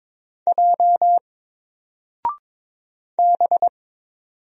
Morse code, heard as a keyed tone, about 22 wpm, 700 Hz. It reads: J B
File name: Single-Letter-Number-22wpm